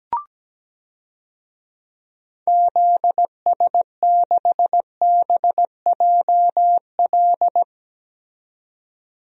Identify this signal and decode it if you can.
Morse code, heard as a keyed tone, about 17 wpm, 700 Hz. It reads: ZS6BJL
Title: DXCC-100 Call Signs - 6 Characters - 17wpm